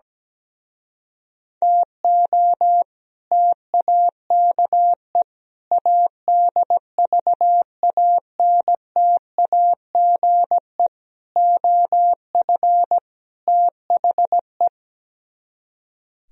Morse code, heard as a keyed tone, about 17 wpm, 700 Hz. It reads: TO TAKE ADVANTAGE OF THE
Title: Speed-Racing - Sets of 5 Words 17wpm